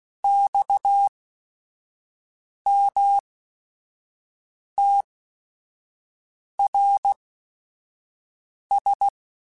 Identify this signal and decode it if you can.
Morse code, heard as a keyed tone, about 16 wpm, spaced as for 5 wpm, 800 Hz. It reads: XMTRS
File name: QSOs / QSOs-mp3s / WA7QOX 1@5